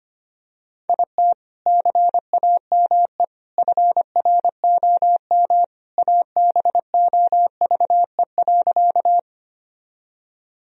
Morse code, heard as a keyed tone, about 25 wpm, 700 Hz. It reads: IT CAME FROM ABOVE.
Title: Sentences from Top 1000 Words - Review Part 2 - 25wpm